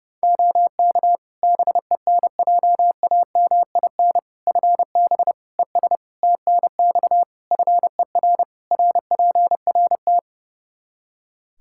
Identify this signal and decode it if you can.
Morse code, heard as a keyed tone, about 30 wpm, 700 Hz. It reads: OK BENJAMIN FB ES TNX FER RPRT
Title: K1USN SST QSO Practice 30wpm